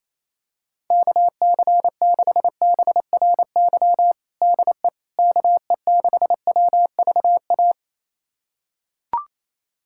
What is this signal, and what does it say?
Morse code, heard as a keyed tone, about 28 wpm, 700 Hz. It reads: KC6BRY DE KE6WVA
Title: USA Callsign DE Callsign - 6 Characters - 28wpm